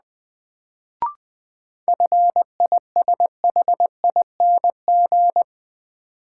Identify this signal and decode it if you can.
Morse code, heard as a keyed tone, about 20 wpm, 700 Hz. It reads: FISHING
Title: Sentences from Top 1000 Words - Review Part 5 - New Word Drill - 20wpm